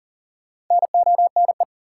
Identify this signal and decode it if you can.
Morse code, heard as a keyed tone, about 40 wpm, 700 Hz. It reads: NONE
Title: Four Letter Words - 40wpm